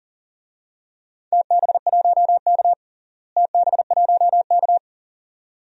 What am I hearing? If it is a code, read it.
Morse code, heard as a keyed tone, about 41 wpm, 700 Hz. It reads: TB1K TB1K
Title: DXCC-100 Call Signs - 4 Characters - Repeated 2x - 40wpm